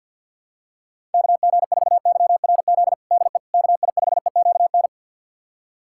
Morse code, heard as a keyed tone, about 53 wpm, 700 Hz. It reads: KG4YRZ DE KI5EQN
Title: USA Callsign DE Callsign - 6 Characters - 50wpm